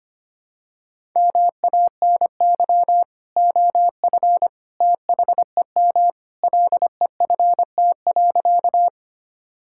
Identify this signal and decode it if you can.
Morse code, heard as a keyed tone, about 25 wpm, 700 Hz. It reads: MANY OF THEM LEFT.
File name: practice / 25wpm / Sentences with 2-Word Phrases - 25wpm